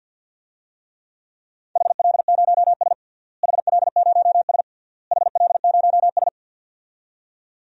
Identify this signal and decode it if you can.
Morse code, heard as a keyed tone, about 50 wpm, 700 Hz. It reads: HL0S HL0S HL0S
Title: DXCC-100 Call Signs - 4 Characters - Repeated 3x - 50wpm